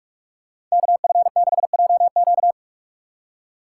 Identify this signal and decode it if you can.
Morse code, heard as a keyed tone, about 40 wpm, 700 Hz. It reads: KF6JQ